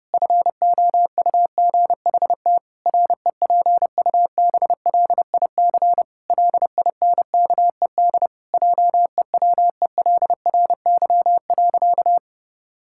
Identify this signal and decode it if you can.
Morse code, heard as a keyed tone, about 32 wpm, 700 Hz. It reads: FOUGHT REPUBLIC LINKED JEWELRY.
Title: Pangram Challenge v2 - 30wpm